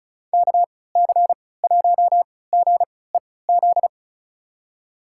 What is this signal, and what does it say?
Morse code, heard as a keyed tone, about 36 wpm, spaced as for 20 wpm, 700 Hz. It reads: KC1GEZ
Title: USA Call Signs - 6 Characters - 3x Character Spacing - 35wpm